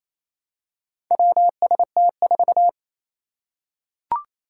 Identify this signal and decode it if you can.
Morse code, heard as a keyed tone, about 29 wpm, 700 Hz. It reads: WST4
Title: Four Challenging Letter-Number Combinations 28wpm